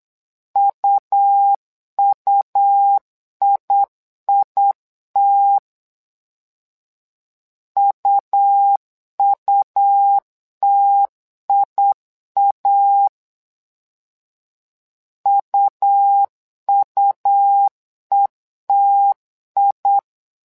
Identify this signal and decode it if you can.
Morse code, heard as a keyed tone, about 8 wpm, 800 Hz. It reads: UUIIT UUTIA UUETI
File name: LEZIONI / 2 UHDN / 2 U 8wpm 2028